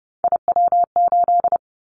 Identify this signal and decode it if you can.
Morse code, heard as a keyed tone, about 31 wpm, 700 Hz. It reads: IW8
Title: Callsign Prefixes for Top 25 DXCC - 30wpm